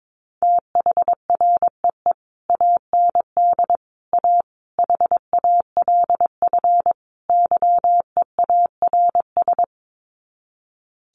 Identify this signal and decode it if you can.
Morse code, heard as a keyed tone, about 22 wpm, 700 Hz. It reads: THREE AND A HALF YEARS
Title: Sets of 5 Words Encore2 22wpm